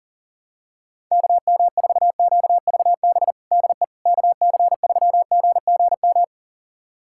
Morse code, heard as a keyed tone, about 40 wpm, 700 Hz. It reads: KM4QVB DE KC3GGM
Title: USA Callsign DE Callsign - 6 Characters - 40wpm